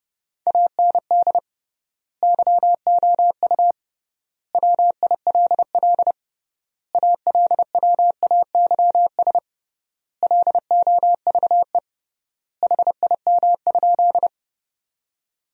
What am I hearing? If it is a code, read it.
Morse code, heard as a keyed tone, about 30 wpm, 700 Hz. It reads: AND YOU WILL ALWAYS LOVE HIM?
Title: 3x Word-Spacing - Sentences from Top 400 Words - 30wpm